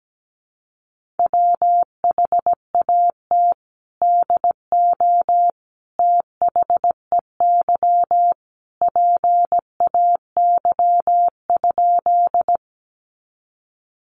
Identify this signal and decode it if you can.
Morse code, heard as a keyed tone, about 17 wpm, 700 Hz. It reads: WHAT DO THEY PAY?
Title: Sentences from Top 700 Words 17wpm